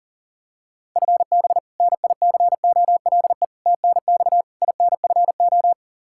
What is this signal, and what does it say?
Morse code, heard as a keyed tone, about 37 wpm, 700 Hz. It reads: FB NICOLE TNX INFO
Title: Practice QSO - Exchange 3 - More Info - 40wpm